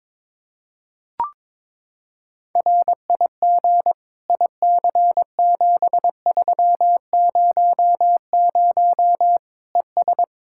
Morse code, heard as a keyed tone, about 22 wpm, 700 Hz. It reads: RIG IC7300 ES
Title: Practice QSO - Exchange 2 - Info - 22wpm